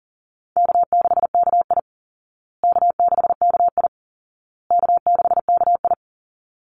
Morse code, heard as a keyed tone, about 40 wpm, 700 Hz. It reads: K6KI K6KI K6KI
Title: Call Signs Repeated 2x - 40wpm